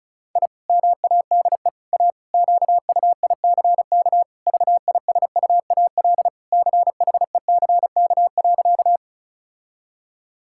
Morse code, heard as a keyed tone, about 36 wpm, 700 Hz. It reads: I MADE A QUICK VISUAL CHECK.
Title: Sentences from Top 1000 Words - Review Part 6 - 35wpm